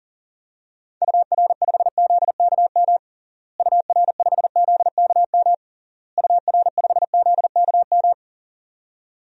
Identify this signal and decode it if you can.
Morse code, heard as a keyed tone, about 40 wpm, 700 Hz. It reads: UR5ZKM UR5ZKM UR5ZKM
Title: DXCC-100 Call Signs - 6 Characters - Repeated 3x - 40wpm